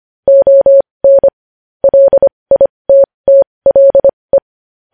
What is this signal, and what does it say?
Morse code, heard as a keyed tone, about 25 wpm, spaced as for 20 wpm, 550 Hz. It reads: ON LITTLE
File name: podcast / News headlines-1x-25x20